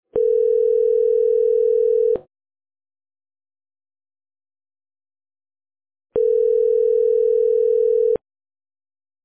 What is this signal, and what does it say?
Ringback tone (North America), 440 + 480 Hz.